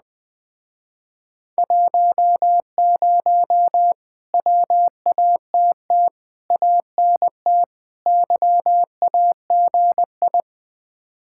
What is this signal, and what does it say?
Morse code, heard as a keyed tone, about 20 wpm, 700 Hz. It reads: 10 WATT ANT YAGI
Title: Speed-Racing Common QSO Phrases Encore - 17wpm